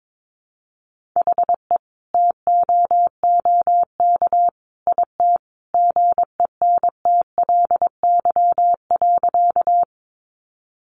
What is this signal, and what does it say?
Morse code, heard as a keyed tone, about 22 wpm, 700 Hz. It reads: HE TOOK IT GENTLY.
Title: Sentences from Top 1500 Words 22wpm